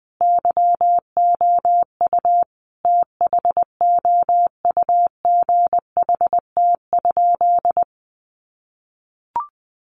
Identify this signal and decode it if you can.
Morse code, heard as a keyed tone, about 20 wpm, 700 Hz. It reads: YOU THOUGHT?
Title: Sentences from Top 300 Words 20wpm